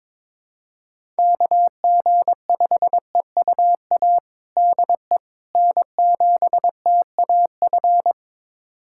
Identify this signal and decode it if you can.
Morse code, heard as a keyed tone, about 22 wpm, 700 Hz. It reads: KG5EUA DE N7TAF
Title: USA Callsign DE Callsign 22wpm